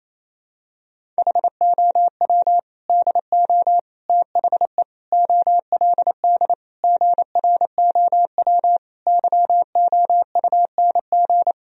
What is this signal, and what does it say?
Morse code, heard as a keyed tone, about 28 wpm, 700 Hz. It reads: HOW DO THE OLD GROW YOUNG
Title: Sentences from Top 1000 Words - Review Part 1 - 28wpm